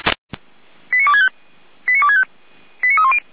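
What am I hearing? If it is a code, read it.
Five-tone selective call (ZVEI), heard as quick runs of five tones, tones of about 80 ms. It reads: 89265 89265 89310
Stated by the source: ZVEI five-tone selective call for a fire brigade, received on an SDR.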